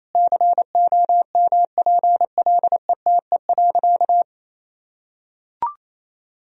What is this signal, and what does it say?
Morse code, heard as a keyed tone, about 28 wpm, 700 Hz. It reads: COMPLETE.
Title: Sentences from Top 1000 Words - Review Part 2 - 28wpm